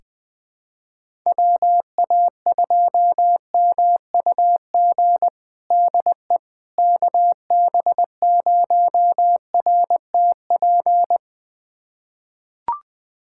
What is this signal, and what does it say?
Morse code, heard as a keyed tone, about 20 wpm, 700 Hz. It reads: WA2MUG DE KB0RTP
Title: USA Callsign DE Callsign - 6 Characters - 20wpm